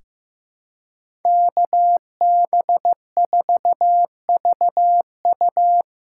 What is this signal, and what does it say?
Morse code, heard as a keyed tone, about 15 wpm, 700 Hz. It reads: KB4VU
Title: K1USN SST QSO Practice - 15wpm